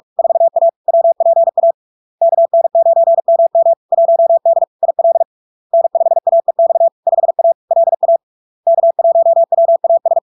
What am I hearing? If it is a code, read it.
Morse code, heard as a keyed tone, about 46 wpm, 670 Hz. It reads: 4A WPA KN9MM 1D IL N5AEX 5A LA K1WAS
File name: LARK-FD-2022-Log-in-CW-45wpm-670-Hz